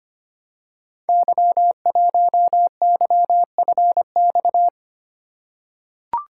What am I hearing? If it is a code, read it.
Morse code, heard as a keyed tone, about 25 wpm, 700 Hz. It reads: Y1YFX
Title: Code Groups - 5 Letter:Numbers - 25wpm